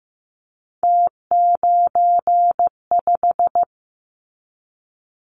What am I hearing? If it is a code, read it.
Morse code, heard as a keyed tone, about 15 wpm, 700 Hz. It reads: T95
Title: Callsign Prefixes for Top 50 DXCC - 15wpm